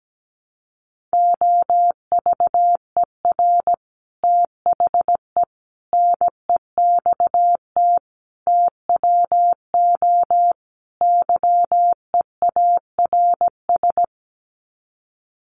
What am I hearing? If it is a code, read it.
Morse code, heard as a keyed tone, about 17 wpm, 700 Hz. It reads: OVER THE NEXT TWO YEARS
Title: Sets of 5 Words Encore2 17wpm